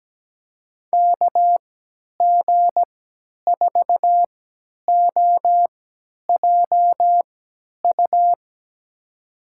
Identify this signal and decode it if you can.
Morse code, heard as a keyed tone, about 17 wpm, spaced as for 10 wpm, 700 Hz. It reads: KG4OJU
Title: USA Call Signs - 6 Characters - 3x Character Spacing - 17wpm